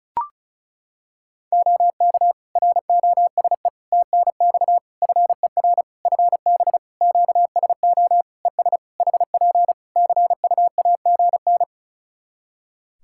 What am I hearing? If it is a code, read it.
Morse code, heard as a keyed tone, about 36 wpm, 700 Hz. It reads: OK ROSE TNX FER FB QSO ES HP CUAGN
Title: Practice QSO - Exchange 4 - Ending - 35wpm